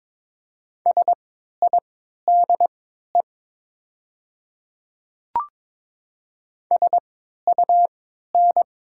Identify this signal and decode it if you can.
Morse code, heard as a keyed tone, about 22 wpm, spaced as for 12 wpm, 700 Hz. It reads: SIDE SUN
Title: 3x Character Spacing - Top 300 Words - 22wpm